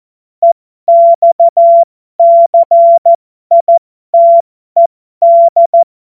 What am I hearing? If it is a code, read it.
Morse code, heard as a keyed tone, about 14 wpm, spaced as for 12 wpm, 680 Hz. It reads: EXCITED
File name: Morsekurs / Witze